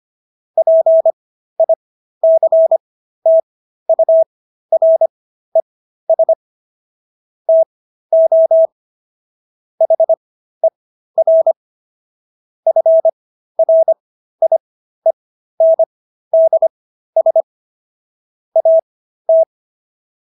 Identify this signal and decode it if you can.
Morse code, heard as a keyed tone, about 25 wpm, spaced as for 13 wpm, 650 Hz. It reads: PICTURES TO HER FRIENDS AT